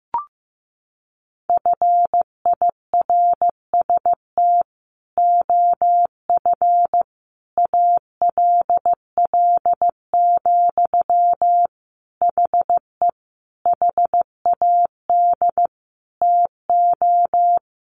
Morse code, heard as a keyed tone, about 15 wpm, 700 Hz. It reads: FIRST OF ALL, HE HAD TO
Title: Sentences from Top 1200 Words 15wpm